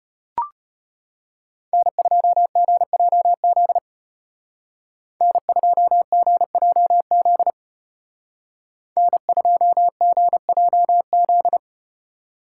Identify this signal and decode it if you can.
Morse code, heard as a keyed tone, about 34 wpm, 700 Hz. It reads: N2GJZ N2GJZ N2GJZ